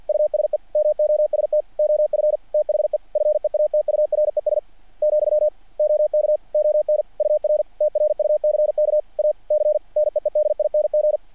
Morse code, heard as a keyed tone, about 48 wpm, 600 Hz. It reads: USE MOST OF THE FEATURES , OK ON UR TRUCK A C NEEDING
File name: archived / CRSnet